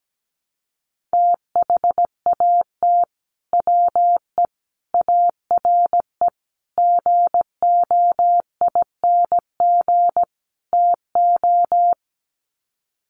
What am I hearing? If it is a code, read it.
Morse code, heard as a keyed tone, about 17 wpm, 700 Hz. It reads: THAT WE ARE GOING TO
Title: Sets of 5 Words Encore 17wpm